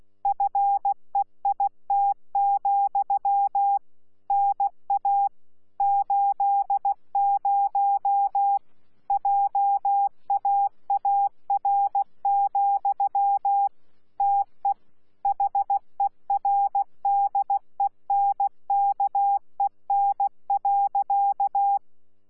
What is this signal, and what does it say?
Morse code, heard as a keyed tone, about 16 wpm, 800 Hz. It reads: FEIT, NA 80 JAAR, TE HERDENKEN.
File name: Z PI4VRZ / PI4VRZ 14-jun-2025